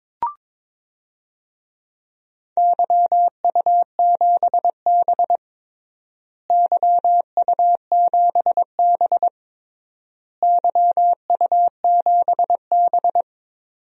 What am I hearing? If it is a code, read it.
Morse code, heard as a keyed tone, about 22 wpm, 700 Hz. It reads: YU7B YU7B YU7B
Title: DXCC-100 Call Signs - 4 Characters - Repeated 3x - 22wpm